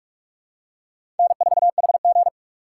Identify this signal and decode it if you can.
Morse code, heard as a keyed tone, about 43 wpm, 700 Hz. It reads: N4HG
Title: USA Call Signs - 4 Characters - 6x Character Spacing - 45wpm